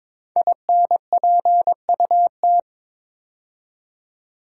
Morse code, heard as a keyed tone, about 22 wpm, 700 Hz. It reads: INPUT
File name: 5 Letter Words - First 3 Letters Given - 22wpm